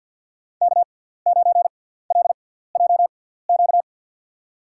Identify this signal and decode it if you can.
Morse code, heard as a keyed tone, about 52 wpm, spaced as for 17 wpm, 700 Hz. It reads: K9LJQ
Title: USA Call Signs - 6x Character Spacing - 50wpm